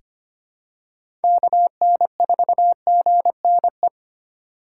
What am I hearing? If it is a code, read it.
Morse code, heard as a keyed tone, about 25 wpm, 700 Hz. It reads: KN4GNE
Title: USA Call Signs - 2x Character Spacing - 25wpm